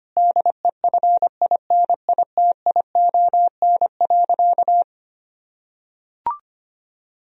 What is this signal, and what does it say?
Morse code, heard as a keyed tone, about 25 wpm, 700 Hz. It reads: DEFINITION.
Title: Sentences from Top 1000 Words - Review Part 4 - 25wpm